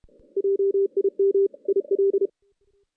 Morse code, heard as a keyed tone, about 31 wpm, 390 Hz. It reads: JIM IL
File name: 2016 JAN CW NAQP / 2016 Jan naqp k5ot - 0-5